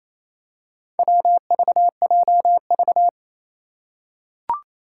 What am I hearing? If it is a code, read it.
Morse code, heard as a keyed tone, about 28 wpm, 700 Hz. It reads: WVJV
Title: Four Challenging Letter-Number Combinations 28wpm